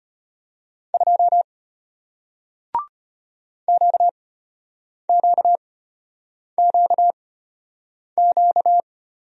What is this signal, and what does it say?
Morse code, heard as a keyed tone, about 34 wpm, spaced as for 8 wpm, 700 Hz. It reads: 2 QQQQ